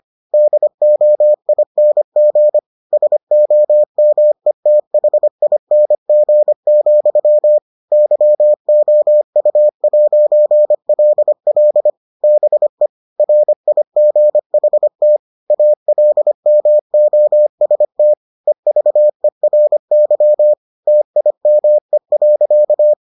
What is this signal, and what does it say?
Morse code, heard as a keyed tone, about 25 wpm, 600 Hz. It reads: DOING SOMETHING, YOU'LL BE RIGHT ALMOST EVERY TIME.